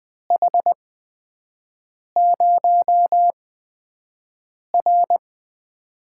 Morse code, heard as a keyed tone, about 20 wpm, spaced as for 5 wpm, 700 Hz. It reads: H0R
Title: USA Call Signs - 8x Character Spacing - 20wpm